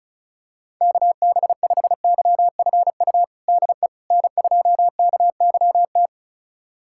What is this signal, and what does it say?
Morse code, heard as a keyed tone, about 34 wpm, 700 Hz. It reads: KB5YFU DE N2KYT
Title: USA Callsign DE Callsign 35wpm